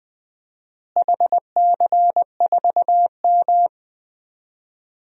Morse code, heard as a keyed tone, about 20 wpm, 700 Hz. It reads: HC4M
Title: DXCC-100 Call Signs - 4 Characters - Repeated 3x - 20wpm